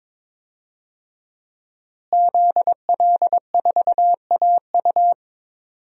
Morse code, heard as a keyed tone, about 21 wpm, 700 Hz. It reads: ZL4AU
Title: DXCC-100 Call Signs - 5 Characters - Repeated 2x - 22wpm